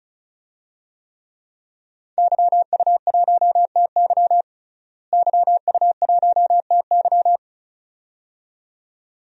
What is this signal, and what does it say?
Morse code, heard as a keyed tone, about 36 wpm, 700 Hz. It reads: YU1TY YU1TY
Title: DXCC-100 Call Signs - 5 Characters - Repeated 2x - 35wpm